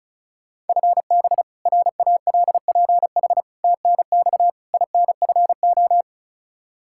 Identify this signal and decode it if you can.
Morse code, heard as a keyed tone, about 36 wpm, 700 Hz. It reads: FB RALPH TNX INFO
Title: Practice QSO - Exchange 3 - More Info - 35wpm